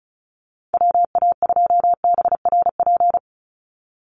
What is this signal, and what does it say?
Morse code, heard as a keyed tone, about 34 wpm, 700 Hz. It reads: WA2BRP
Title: USA Call Signs 35wpm